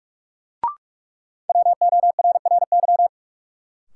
Morse code, heard as a keyed tone, about 46 wpm, 700 Hz. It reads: WORRY
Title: Sentences from Top 1100 Words - New Word Drill Alt - 45wpm